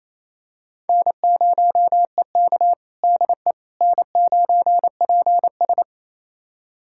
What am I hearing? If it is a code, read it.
Morse code, heard as a keyed tone, about 28 wpm, 700 Hz. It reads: N0EK DE N9PS
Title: USA Callsign DE Callsign - 4 Characters - 28wpm